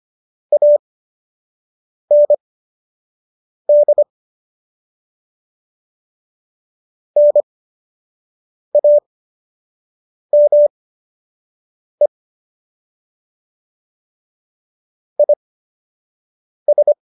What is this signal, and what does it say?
Morse code, heard as a keyed tone, about 25 wpm, spaced as for 6 wpm, 590 Hz. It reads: AND NAME IS